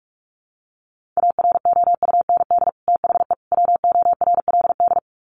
Morse code, heard as a keyed tone, about 44 wpm, 700 Hz. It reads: AROUND THE WORLD
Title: Sets of 3 Words 45wpm